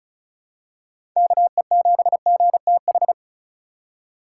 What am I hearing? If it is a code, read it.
Morse code, heard as a keyed tone, about 34 wpm, 700 Hz. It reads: KE7GTH
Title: USA Call Signs - 6 Characters - 2x Character Spacing - 35wpm